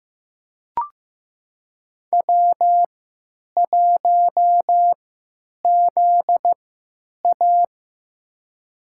Morse code, heard as a keyed tone, about 15 wpm, spaced as for 9 wpm, 700 Hz. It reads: W1ZA
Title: USA Call Signs - 3x Character Spacing - 15wpm